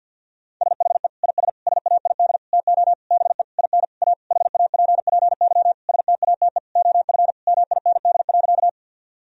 Morse code, heard as a keyed tone, about 52 wpm, 700 Hz. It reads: SHE IS SAID TO BE IN A HAPPY STATE OF MIND.
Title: Sentences from Top 1000 Words - Review Part 2 - 50wpm